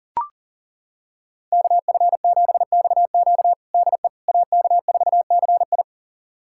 Morse code, heard as a keyed tone, about 40 wpm, 700 Hz. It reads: KF7XQ DE AK4CI
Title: USA Callsign DE Callsign - 5 Characters - 40wpm